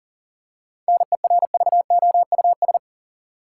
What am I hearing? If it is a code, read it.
Morse code, heard as a keyed tone, about 40 wpm, 700 Hz. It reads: NERVOUS
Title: Seven Letter Words - 40wpm